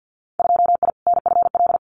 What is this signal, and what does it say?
Morse code, heard as a keyed tone, about 50 wpm, 700 Hz. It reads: 2I NFL
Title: Winter Field Day Exchanges 50wpm